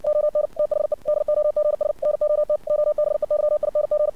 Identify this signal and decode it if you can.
Morse code, heard as a keyed tone, about 60 wpm, 600 Hz. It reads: ON THE DOGS NOT OBEYING
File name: archived / CRSnet